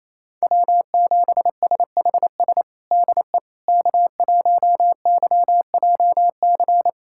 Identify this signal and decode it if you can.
Morse code, heard as a keyed tone, about 28 wpm, 700 Hz. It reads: W7SHS DE K1YJC
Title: USA Callsign DE Callsign - 5 Characters - 28wpm